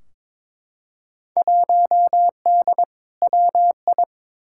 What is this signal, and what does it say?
Morse code, heard as a keyed tone, about 22 wpm, 700 Hz. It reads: 1D WI
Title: Field Day Exchanges - 22wpm